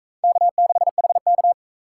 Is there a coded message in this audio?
Yes: Morse code from 0.2 to 1.5 s, heard as a keyed tone, about 36 wpm, 700 Hz. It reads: K6HK